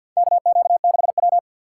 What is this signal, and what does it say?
Morse code, heard as a keyed tone, about 52 wpm, 700 Hz. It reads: KQ6W
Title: USA Call Signs - 2x Character Spacing - 50wpm